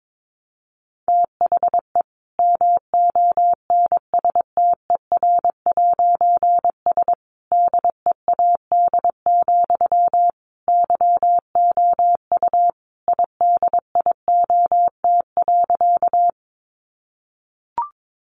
Morse code, heard as a keyed tone, about 22 wpm, 700 Hz. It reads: THE MONSTER'S DEAD, YOU IDIOT.
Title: A Sound Of Thunder 22wpm